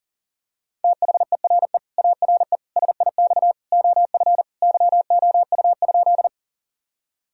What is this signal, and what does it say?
Morse code, heard as a keyed tone, about 40 wpm, 700 Hz. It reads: THERE ARE SIX OF YOU?
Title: Sentences from Top 1000 Words - Review Part 2 - 40wpm